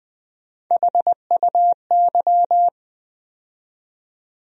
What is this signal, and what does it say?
Morse code, heard as a keyed tone, about 20 wpm, 700 Hz. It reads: HUY
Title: USA Call Sign Postfixes - 3 Letters 20wpm